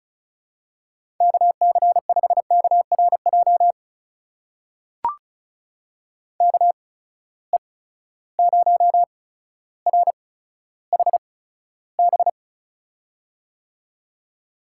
Morse code, heard as a keyed tone, about 36 wpm, spaced as for 15 wpm, 700 Hz. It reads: KC5KRJ KE0RHB
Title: USA Call Signs - 6 Characters - 8x Character Spacing - 35wpm